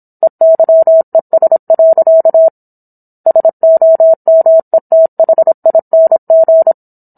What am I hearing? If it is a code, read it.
Morse code, heard as a keyed tone, about 27 wpm, 650 Hz. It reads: EYES. SOMETHING